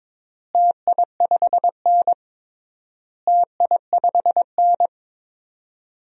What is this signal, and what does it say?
Morse code, heard as a keyed tone, about 22 wpm, 700 Hz. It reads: TI5N TI5N
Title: DXCC-100 Call Signs - 4 Characters - Repeated 3x - 22wpm